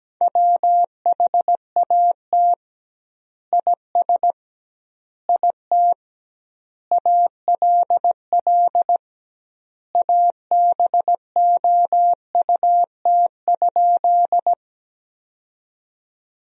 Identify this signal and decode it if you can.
Morse code, heard as a keyed tone, about 17 wpm, 700 Hz. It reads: WHAT IS IT ALL ABOUT?
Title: 2x Word-Spacing - Sentences from Top 100 Words - 17wpm